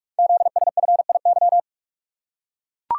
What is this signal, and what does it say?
Morse code, heard as a keyed tone, about 44 wpm, 700 Hz. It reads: ZSFIY
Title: Code Groups - 5 Letters - 45wpm